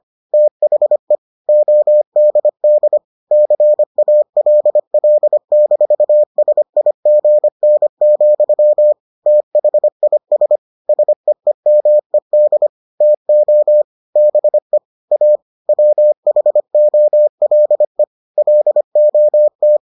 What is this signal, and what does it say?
Morse code, heard as a keyed tone, about 25 wpm, 600 Hz. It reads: THE ODD CALL-SIGN, THIS SEEMED TO BE A WHOLE LOT